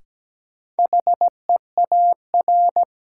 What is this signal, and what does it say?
Morse code, heard as a keyed tone, about 17 wpm, 700 Hz. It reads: HEAR
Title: Sentences from Top 300 Words - New Word Drill - 17wpm